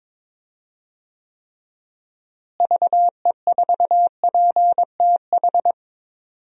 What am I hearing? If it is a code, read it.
Morse code, heard as a keyed tone, about 22 wpm, 700 Hz. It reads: VE4PTH
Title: DXCC-100 Call Signs - 6 Characters - Repeated 2x - 22wpm